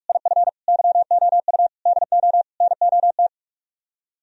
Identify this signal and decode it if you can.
Morse code, heard as a keyed tone, about 49 wpm, 700 Hz. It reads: IF YOU DO NOT